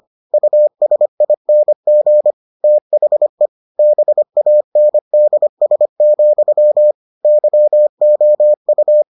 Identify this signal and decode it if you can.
Morse code, heard as a keyed tone, about 25 wpm, 600 Hz. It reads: USING THE BANDS, YOU